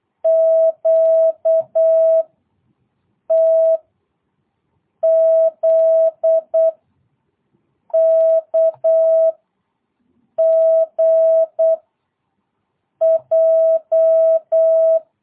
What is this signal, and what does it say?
Morse code, heard as a keyed tone, about 8 wpm, 660 Hz. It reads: Q T Z K G J
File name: jh8cbh / 2022cw